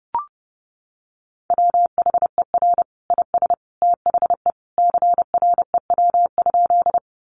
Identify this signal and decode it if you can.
Morse code, heard as a keyed tone, about 30 wpm, 700 Hz. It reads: WHER IS THE CREW?
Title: Sentences from Top 1500 Words 30wpm